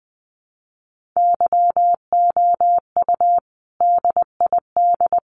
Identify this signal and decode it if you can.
Morse code, heard as a keyed tone, about 20 wpm, 700 Hz. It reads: YOU DID
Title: Sets of 5 Words Encore 20wpm